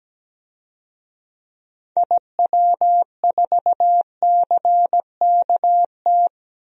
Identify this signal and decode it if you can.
Morse code, heard as a keyed tone, about 17 wpm, 700 Hz. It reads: IW4CKT
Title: DXCC-100 Call Signs - 6 Characters - 17wpm